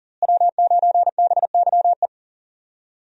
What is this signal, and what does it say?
Morse code, heard as a keyed tone, about 40 wpm, 700 Hz. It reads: W9BYE
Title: USA Call Signs - 5 Characters - Repeated 3x - 40wpm